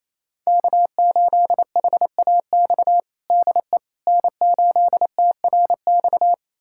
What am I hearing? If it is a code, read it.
Morse code, heard as a keyed tone, about 28 wpm, 700 Hz. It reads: K8HAX DE N8TRX
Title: USA Callsign DE Callsign - 5 Characters - 28wpm